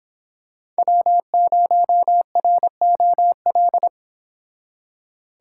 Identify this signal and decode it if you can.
Morse code, heard as a keyed tone, about 26 wpm, 700 Hz. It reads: W0ROL